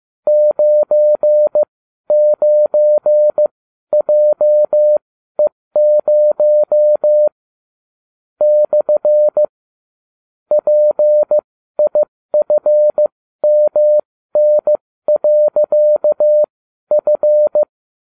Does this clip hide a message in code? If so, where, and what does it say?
Yes: Morse code from 0.3 to 17.6 s, heard as a keyed tone, about 15 wpm, spaced as for 12 wpm, 600 Hz. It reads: 99JE0 / PIFMN.F